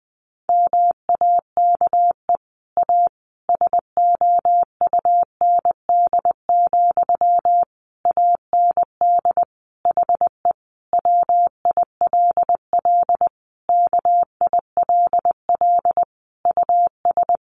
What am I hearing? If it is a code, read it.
Morse code, heard as a keyed tone, about 20 wpm, 700 Hz. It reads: MAKE A SOUND, AND HE WILL KILL US
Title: Sentences from Top 1400 Words 20wpm